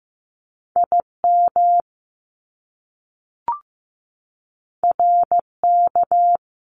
Morse code, heard as a keyed tone, about 15 wpm, 700 Hz. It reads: IM RK
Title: Top Two Letter Combinations in Words 15wpm